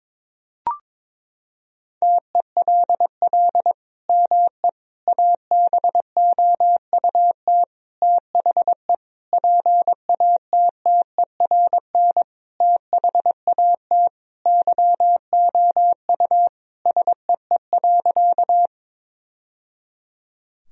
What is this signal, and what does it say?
Morse code, heard as a keyed tone, about 22 wpm, 700 Hz. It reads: TELL ME ABOUT THE PATTERN THAT YOU SEE.
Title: Sentences from Top 1000 Words - Review Part 2 - 22wpm